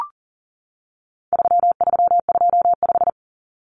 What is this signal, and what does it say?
Morse code, heard as a keyed tone, about 40 wpm, 700 Hz. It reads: 3325
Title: Common Cut Numbers - 4-Digit 40wpm